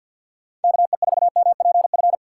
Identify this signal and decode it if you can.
Morse code, heard as a keyed tone, about 53 wpm, 700 Hz. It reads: KE4MPF
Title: USA Call Signs - 6 Characters - 8x Character Spacing - 50wpm